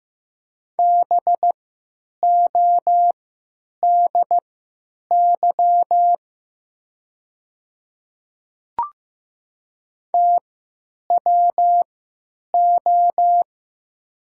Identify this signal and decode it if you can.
Morse code, heard as a keyed tone, about 15 wpm, spaced as for 9 wpm, 700 Hz. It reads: BODY TWO
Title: 3x Character Spacing - Top 500 Words - 15wpm